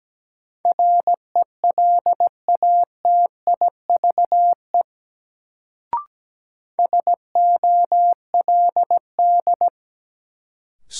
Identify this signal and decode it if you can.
Morse code, heard as a keyed tone, about 17 wpm, 700 Hz. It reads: RELATIVE SOLD
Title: Sentences from Top 1400 Words - New Word Drill Alt - 17wpm